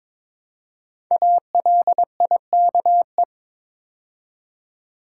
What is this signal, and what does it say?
Morse code, heard as a keyed tone, about 23 wpm, 700 Hz. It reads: ALIKE
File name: 5 Letter Words - First 3 Letters Given - 22wpm